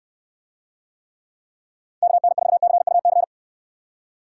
Morse code, heard as a keyed tone, about 69 wpm, 700 Hz. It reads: KN4CUC